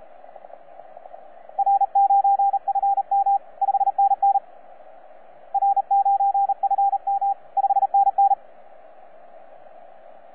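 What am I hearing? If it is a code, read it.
Morse code, heard as a keyed tone, about 36 wpm, 790 Hz, with radio noise. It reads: R9FM 5NN R9FM 5NN
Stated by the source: T47RRC, IOTA NA-086, on 15 m CW, a DX recording by DX World.net.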